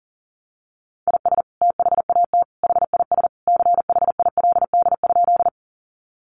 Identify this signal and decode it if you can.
Morse code, heard as a keyed tone, about 40 wpm, 700 Hz. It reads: IS THAT HIS CHILD?
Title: Sentences from Top 800 Words 40wpm